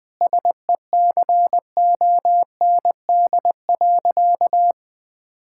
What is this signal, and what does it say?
Morse code, heard as a keyed tone, about 21 wpm, 700 Hz. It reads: SECOND.
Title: Difficult Sentences with Hints - 20wpm